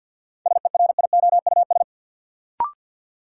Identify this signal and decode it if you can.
Morse code, heard as a keyed tone, about 50 wpm, 700 Hz. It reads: SERIOUS
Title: Sentences from Top 1100 Words - New Word Drill Alt - 50wpm